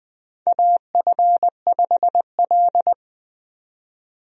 Morse code, heard as a keyed tone, about 20 wpm, 700 Hz. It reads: AF5L
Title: USA Call Signs - 4 Characters - 20wpm